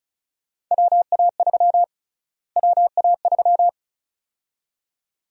Morse code, heard as a keyed tone, about 34 wpm, 700 Hz. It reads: WA3 WA3
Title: USA Call Sign Prefixes - Repeated 2x - 35wpm